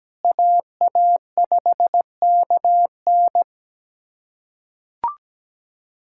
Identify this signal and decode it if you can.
Morse code, heard as a keyed tone, about 17 wpm, 700 Hz. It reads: AA5KN
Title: USA Callsign DE Callsign - 5 Characters - 17wpm